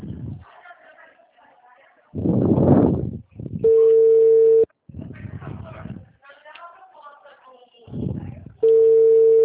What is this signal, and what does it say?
Ringback tone (China, 450 Hz), with speech.
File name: out- -2004- - - .57128